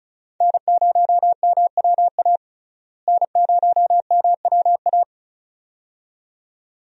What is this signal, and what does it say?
Morse code, heard as a keyed tone, about 34 wpm, 700 Hz. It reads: N0MWA N0MWA
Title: USA Call Signs - 5 Characters - Repeated 2x - 35wpm